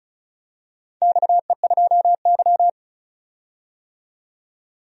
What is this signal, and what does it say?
Morse code, heard as a keyed tone, about 36 wpm, 700 Hz. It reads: XE2Y
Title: DXCC-100 Call Signs - 4 Characters - 35wpm